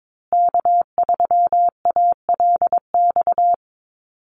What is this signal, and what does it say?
Morse code, heard as a keyed tone, about 22 wpm, 700 Hz. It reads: K3ALX
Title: USA Call Signs Repeated 3x 22wpm